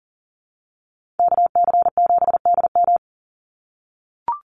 Morse code, heard as a keyed tone, about 40 wpm, 700 Hz. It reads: KC7DM
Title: NA SOTA Activator Callsigns 40wpm